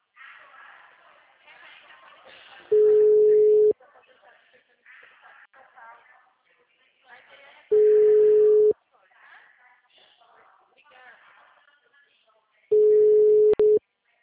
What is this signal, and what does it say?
Ringback tone (Europe, 425 Hz), with speech.
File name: out- -2039- - - .13462